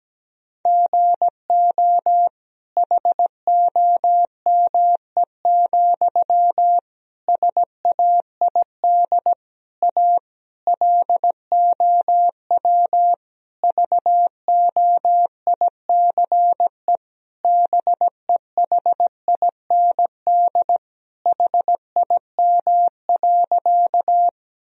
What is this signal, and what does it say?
Morse code, heard as a keyed tone, about 17 wpm, 700 Hz. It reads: GO HOME, SAID A LOW VOICE BEHIND HIM.
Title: Sentences from Top 1000 Words - Review Part 1 - 17wpm